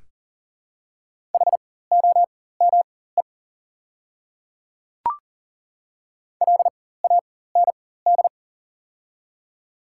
Morse code, heard as a keyed tone, about 40 wpm, spaced as for 19 wpm, 700 Hz. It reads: HOME LAND
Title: 4x Character Spacing - Top 200 Words - 40wpm